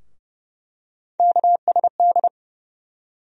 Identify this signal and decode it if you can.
Morse code, heard as a keyed tone, about 30 wpm, 700 Hz. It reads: KSD
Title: USA Call Sign Postfixes - 3 Letters 30wpm